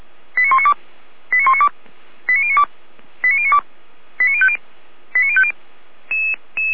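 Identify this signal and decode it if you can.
Five-tone selective call (ZVEI), heard as quick runs of five tones, tones of about 80 ms. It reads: 89192 89192 89092 89092 89050 89050
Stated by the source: ZVEI five-tone selective call 89172, received by SDR.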